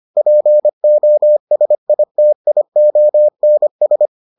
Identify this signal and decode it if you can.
Morse code, heard as a keyed tone, about 25 wpm, 600 Hz. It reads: POSITIONS